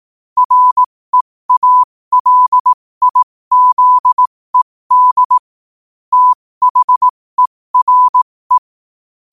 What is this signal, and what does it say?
Morse code, heard as a keyed tone, about 18 wpm, spaced as for 15 wpm, 1000 Hz. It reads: REALIZED THERE
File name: qotdfiles / 15wpm